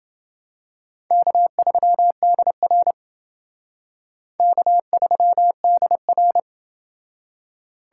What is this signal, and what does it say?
Morse code, heard as a keyed tone, about 27 wpm, 700 Hz. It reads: K3DR K3DR